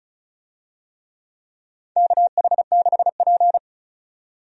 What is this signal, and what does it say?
Morse code, heard as a keyed tone, about 34 wpm, 700 Hz. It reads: KH6P
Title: DXCC-100 Call Signs - 4 Characters - 35wpm